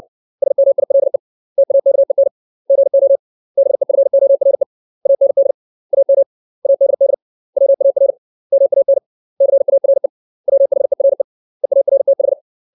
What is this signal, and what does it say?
Morse code, heard as a keyed tone, about 60 wpm, 550 Hz. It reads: SMILE TAKEN GO BEFORE AND AM ADD WAR MAN CARE WHERE EARTH
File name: 632 words / 632 Words 60WPM